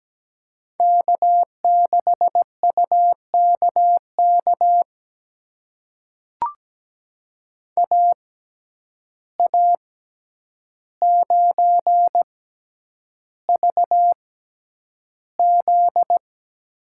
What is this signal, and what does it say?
Morse code, heard as a keyed tone, about 17 wpm, spaced as for 9 wpm, 700 Hz. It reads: K6UKK AA9VZ
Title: USA Call Signs - 5 Characters - 6x Character Spacing - 17wpm